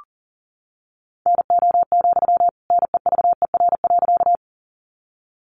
Morse code, heard as a keyed tone, about 40 wpm, 700 Hz. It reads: NO, NEVER.
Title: Sentences from Top 300 Words 40wpm